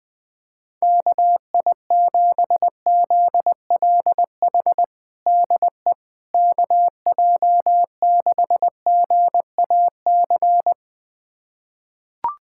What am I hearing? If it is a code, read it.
Morse code, heard as a keyed tone, about 21 wpm, 700 Hz. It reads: KI7ZLH DE KJ6GAC
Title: USA Callsign DE Callsign - 6 Characters - 20wpm